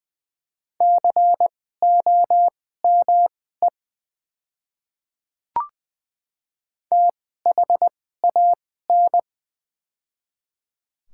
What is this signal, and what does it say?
Morse code, heard as a keyed tone, about 20 wpm, spaced as for 14 wpm, 700 Hz. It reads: COME THAN
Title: 2x Character Spacing - Top 300 Words - 20wpm